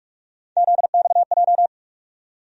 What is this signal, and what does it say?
Morse code, heard as a keyed tone, about 46 wpm, 700 Hz. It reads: ZXJ
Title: USA Call Sign Postfixes - 3 Letters 45wpm